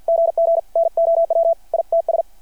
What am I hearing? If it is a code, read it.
Morse code, heard as a keyed tone, about 53 wpm, 660 Hz. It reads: GG NOW ITS
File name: archived / CRSnet